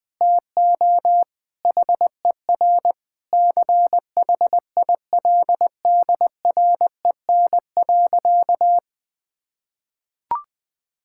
Morse code, heard as a keyed tone, about 21 wpm, 700 Hz. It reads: TO HER CHILDREN.
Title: Sentences from Top 1000 Words - Review Part 5 - 20wpm